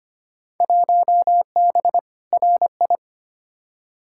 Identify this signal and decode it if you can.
Morse code, heard as a keyed tone, about 25 wpm, 700 Hz. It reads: 1B RI
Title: Field Day Exchanges Encore - 25wpm